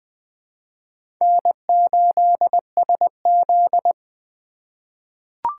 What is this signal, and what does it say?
Morse code, heard as a keyed tone, about 20 wpm, 700 Hz. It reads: N8SZ
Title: USA Call Signs - 4 Characters - 3x Character Spacing - 20wpm